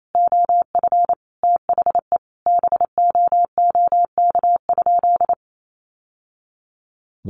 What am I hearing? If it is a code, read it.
Morse code, heard as a keyed tone, about 28 wpm, 700 Hz. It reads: OF THE BOOK?
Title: Sentences from Top 1200 Words 28wpm